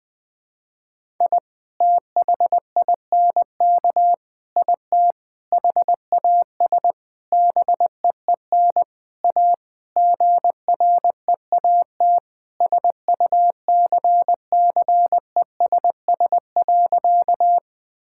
Morse code, heard as a keyed tone, about 20 wpm, 700 Hz. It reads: I THINK IT HAS BEEN A GREAT SUCCESS.
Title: Sentences from Top 1000 Words - Review Part 6 - 20wpm